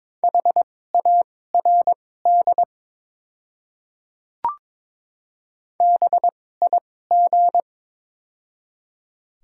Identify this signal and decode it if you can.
Morse code, heard as a keyed tone, about 22 wpm, spaced as for 16 wpm, 700 Hz. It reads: HARD BIG
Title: 2x Character Spacing - Top 300 Words - 22wpm